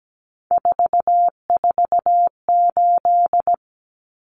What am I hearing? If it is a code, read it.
Morse code, heard as a keyed tone, about 17 wpm, 700 Hz. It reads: 448
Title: Common Cut Numbers - 3-Digit 17wpm